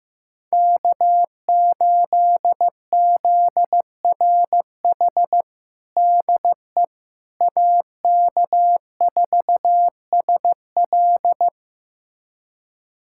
Morse code, heard as a keyed tone, about 15 wpm, 700 Hz. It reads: K8ZRH DE AK4SL
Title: USA Callsign DE Callsign - 5 Characters - 15wpm